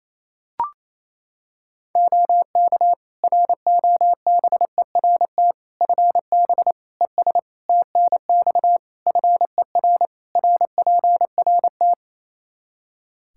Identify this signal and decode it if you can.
Morse code, heard as a keyed tone, about 28 wpm, 700 Hz. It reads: OK ROBERT FB ES TNX FER RPRT
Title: K1USN SST QSO Practice 28wpm